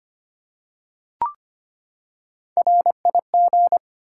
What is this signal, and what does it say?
Morse code, heard as a keyed tone, about 25 wpm, 700 Hz. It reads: RIG
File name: practice / 25wpm / Top 20 QSO Elements - 25wpm